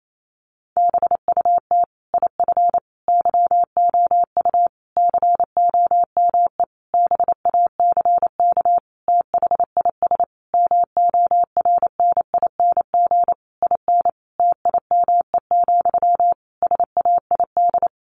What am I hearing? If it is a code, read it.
Morse code, heard as a keyed tone, about 28 wpm, 700 Hz. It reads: BUT IF YOU COME BACK THIS MORNING IN TIME, SAID
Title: A Sound Of Thunder 28wpm